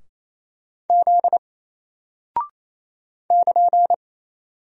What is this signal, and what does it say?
Morse code, heard as a keyed tone, about 28 wpm, 700 Hz. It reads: Z <KN>
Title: Single Character (Letter:Number:Punctuation:Prosign) 28wpm